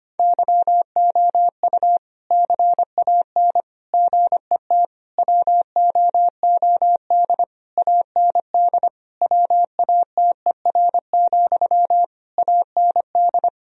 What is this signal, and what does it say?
Morse code, heard as a keyed tone, about 25 wpm, 700 Hz. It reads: YOU CAN GET WOOD AND WATER, AND
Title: Sentences from Top 1000 Words - Review Part 2 - 25wpm